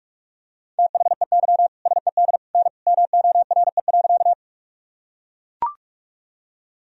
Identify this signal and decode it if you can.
Morse code, heard as a keyed tone, about 43 wpm, 700 Hz. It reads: THEY SED N MORE.
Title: Sentences from Top 100 Words and Abbreviations 45wpm